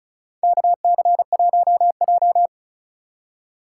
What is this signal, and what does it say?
Morse code, heard as a keyed tone, about 34 wpm, 700 Hz. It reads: KC1J
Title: USA Call Signs - 4 Characters - 6x Character Spacing - 35wpm